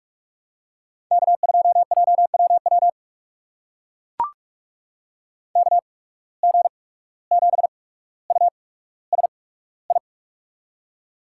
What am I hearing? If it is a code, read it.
Morse code, heard as a keyed tone, about 44 wpm, spaced as for 18 wpm, 700 Hz. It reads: K2JWW KG7USI
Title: USA Call Signs - 8x Character Spacing - 45wpm